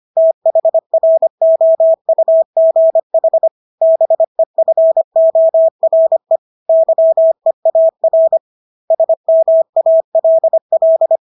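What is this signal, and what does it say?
Morse code, heard as a keyed tone, about 25 wpm, 650 Hz. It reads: THROUGH BEFORE YEAR SMALL
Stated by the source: Morse practice of English words at 25 wpm from CW med Gnister.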